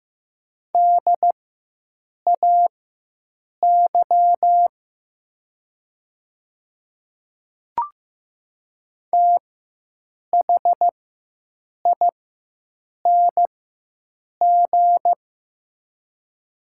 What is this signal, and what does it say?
Morse code, heard as a keyed tone, about 15 wpm, spaced as for 7 wpm, 700 Hz. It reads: DAY THING
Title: 4x Character Spacing - Top 200 Words - 15wpm